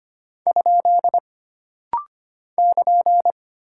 Morse code, heard as a keyed tone, about 25 wpm, 700 Hz. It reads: ? <KN>
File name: Single Punctuation-Prosign - Rapid-Fire - 25wpm